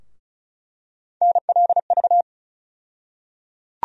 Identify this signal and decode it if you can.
Morse code, heard as a keyed tone, about 34 wpm, 700 Hz. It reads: NLV
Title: USA Call Sign Postfixes - 3 Letters 35wpm